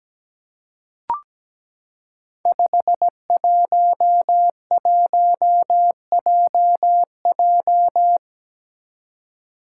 Morse code, heard as a keyed tone, about 17 wpm, 700 Hz. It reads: 511JJ
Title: Five Challenging Letter-Number Combinations 17wpm